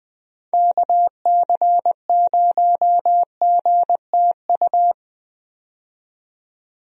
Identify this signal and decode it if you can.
Morse code, heard as a keyed tone, about 20 wpm, 700 Hz. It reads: KC0GTU
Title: USA Call Signs - 6 Characters - 20wpm